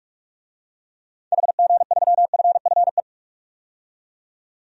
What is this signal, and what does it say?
Morse code, heard as a keyed tone, about 46 wpm, 700 Hz. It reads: HG3FFE
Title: DXCC-100 Call Signs - 6 Characters - 45wpm